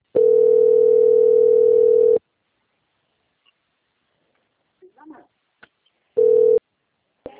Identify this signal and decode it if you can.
Ringback tone (North America), 440 + 480 Hz.